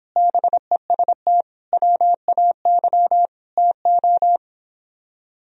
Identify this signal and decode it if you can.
Morse code, heard as a keyed tone, about 26 wpm, 700 Hz. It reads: BEST WAY TO